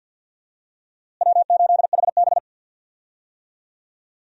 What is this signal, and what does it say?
Morse code, heard as a keyed tone, about 49 wpm, 700 Hz. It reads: W8HB
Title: USA Call Signs - 4 Characters - Repeated 3x - 50wpm